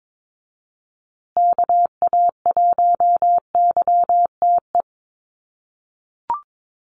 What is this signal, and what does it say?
Morse code, heard as a keyed tone, about 22 wpm, 700 Hz. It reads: KA1YTE
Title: USA Call Signs Repeated 3x 22wpm